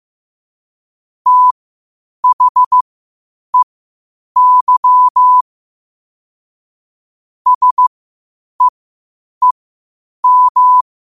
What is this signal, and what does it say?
Morse code, heard as a keyed tone, about 15 wpm, spaced as for 8 wpm, 1000 Hz. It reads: THEY SEEM
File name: qotdfiles / 8wpm